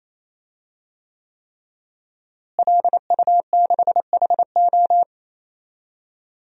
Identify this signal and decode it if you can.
Morse code, heard as a keyed tone, about 27 wpm, 700 Hz. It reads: LU6HO
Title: DXCC-100 Call Signs - 5 Characters - Repeated 3x - 28wpm